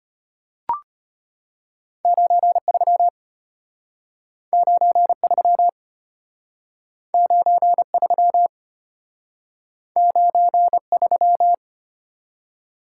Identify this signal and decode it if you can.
Morse code, heard as a keyed tone, about 33 wpm, 700 Hz. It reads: 93 93 93 93